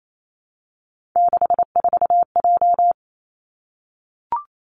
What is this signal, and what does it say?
Morse code, heard as a keyed tone, about 28 wpm, 700 Hz. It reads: 64J
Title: Three Challenging Letter-Number Combinations 28wpm